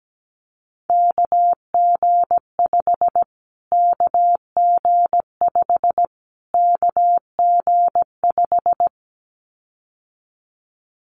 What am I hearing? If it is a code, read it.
Morse code, heard as a keyed tone, about 17 wpm, 700 Hz. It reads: KG5 KG5 KG5
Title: Callsign Prefixes for Top 25 DXCC - Repeated 3x - 17wpm